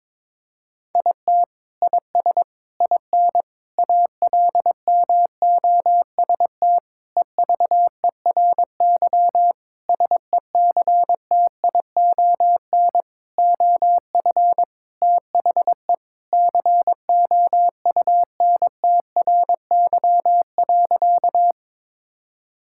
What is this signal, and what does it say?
Morse code, heard as a keyed tone, about 22 wpm, 700 Hz. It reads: IT IS IN ALMOST EVERY SECTION OF THE COUNTRY.
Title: Sentences from Top 1000 Words - Review Part 5 - 22wpm